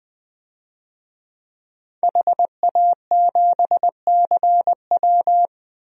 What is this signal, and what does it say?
Morse code, heard as a keyed tone, about 21 wpm, 700 Hz. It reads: HA7CW
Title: DXCC-100 Call Signs - 5 Characters - Repeated 3x - 20wpm